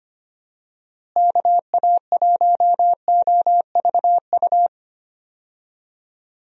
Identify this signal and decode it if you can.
Morse code, heard as a keyed tone, about 25 wpm, 700 Hz. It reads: KA1OVU
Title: USA Call Signs - 6 Characters - Repeated 3x - 25wpm